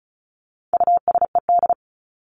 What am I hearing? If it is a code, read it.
Morse code, heard as a keyed tone, about 34 wpm, 700 Hz. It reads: USED
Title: Top 501-750 Words 35wpm